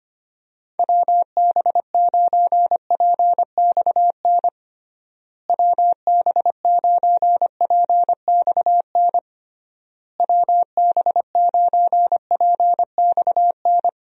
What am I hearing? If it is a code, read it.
Morse code, heard as a keyed tone, about 25 wpm, 700 Hz. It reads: WB9PXN WB9PXN WB9PXN
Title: USA Call Signs - 6 Characters - Repeated 3x - 25wpm